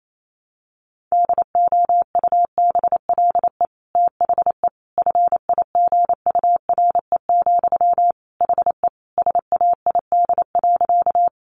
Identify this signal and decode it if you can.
Morse code, heard as a keyed tone, about 28 wpm, 700 Hz. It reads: DOUBLE THE FIGURE, HE SAID.
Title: Sentences from Top 1400 Words 28wpm